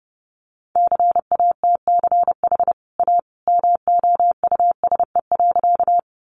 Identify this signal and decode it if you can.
Morse code, heard as a keyed tone, about 30 wpm, 700 Hz. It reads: CATCH A MOUSE.
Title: Sentences from Top 1500 Words 30wpm